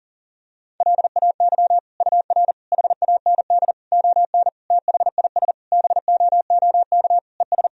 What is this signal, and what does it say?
Morse code, heard as a keyed tone, about 40 wpm, 700 Hz. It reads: LAY UR HAND ON THIS BOOK ES
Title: Sentences from Top 400 Words and Abbreviations 40wpm